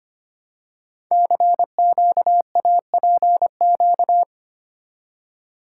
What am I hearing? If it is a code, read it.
Morse code, heard as a keyed tone, about 25 wpm, 700 Hz. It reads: CQAPQ
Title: Code Groups - 5 Letters - 25wpm